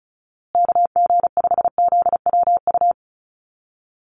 Morse code, heard as a keyed tone, about 34 wpm, 700 Hz. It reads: KG5ZWU
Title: USA Call Signs Repeated 3x 35wpm